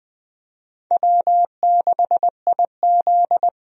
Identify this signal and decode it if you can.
Morse code, heard as a keyed tone, about 20 wpm, 700 Hz. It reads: W6IZ
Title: USA Call Signs - 4 Characters - 20wpm